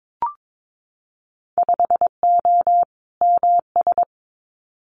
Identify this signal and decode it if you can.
Morse code, heard as a keyed tone, about 22 wpm, 700 Hz. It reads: 5O MS
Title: Winter Field Day Exchanges Encore 22wpm